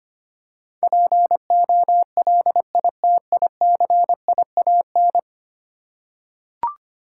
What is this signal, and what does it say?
Morse code, heard as a keyed tone, about 25 wpm, 700 Hz. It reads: POLITICIAN
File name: 10 Letter Words - First 3 Letters Given - 25wpm